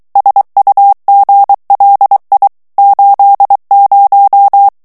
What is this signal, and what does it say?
Morse code, heard as a keyed tone, about 23 wpm, 800 Hz. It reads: SUGLI 80
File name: morse mp3 / ITALIEN texte clair 1 vitesse 22 mots minute